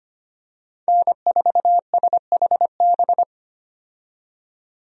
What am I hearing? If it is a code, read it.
Morse code, heard as a keyed tone, about 25 wpm, 700 Hz. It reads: N4SHB
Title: USA Call Signs - 5 Characters - 25wpm